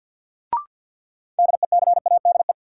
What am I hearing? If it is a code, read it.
Morse code, heard as a keyed tone, about 50 wpm, 700 Hz. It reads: DECADE
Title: Sets of 5 Words - New Word Drill - 50wpm